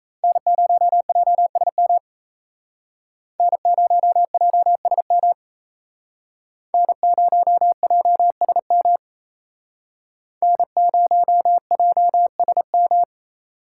Morse code, heard as a keyed tone, about 35 wpm, 700 Hz. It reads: N0JSM N0JSM N0JSM N0JSM
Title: Speed Racing - USA Call Signs 28wpm